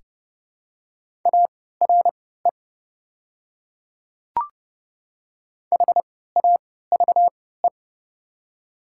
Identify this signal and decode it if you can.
Morse code, heard as a keyed tone, about 29 wpm, spaced as for 17 wpm, 690 Hz. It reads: ARE HAVE
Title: 3x Character Spacing - Top 100 Words - 30wpm